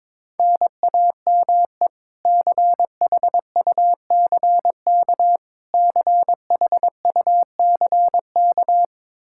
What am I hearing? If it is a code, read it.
Morse code, heard as a keyed tone, about 21 wpm, 700 Hz. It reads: NAME CHUCK CHUCK
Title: K1USN SST QSO Practice 22wpm